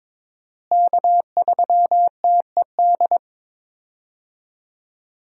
Morse code, heard as a keyed tone, about 22 wpm, 700 Hz. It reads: K3TED
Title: USA Call Signs - 5 Characters - Repeated 2x - 22wpm